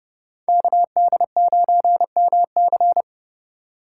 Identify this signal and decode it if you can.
Morse code, heard as a keyed tone, about 30 wpm, 700 Hz. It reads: KD9MC
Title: USA Call Signs - 5 Characters - 4x Character Spacing - 30wpm